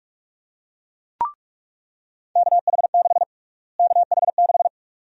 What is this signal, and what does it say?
Morse code, heard as a keyed tone, about 47 wpm, 700 Hz. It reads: KH6 KH6
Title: USA Call Sign Prefixes - Repeated 2x - 45wpm